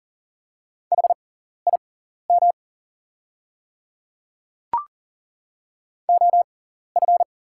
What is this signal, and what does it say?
Morse code, heard as a keyed tone, about 40 wpm, spaced as for 14 wpm, 690 Hz. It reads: HIM OF
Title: 6x Character Spacing - Top 100 Words - 40wpm